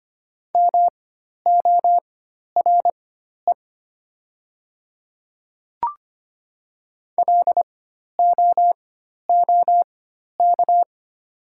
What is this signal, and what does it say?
Morse code, heard as a keyed tone, about 25 wpm, spaced as for 12 wpm, 700 Hz. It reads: MORE LOOK
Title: 4x Character Spacing - Top 100 Words - 25wpm